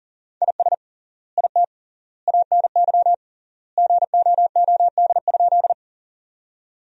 Morse code, heard as a keyed tone, about 40 wpm, 700 Hz. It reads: IS IT ANY GOOD?
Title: 3x Word-Spacing - Sentences from Top 200 Words - 40wpm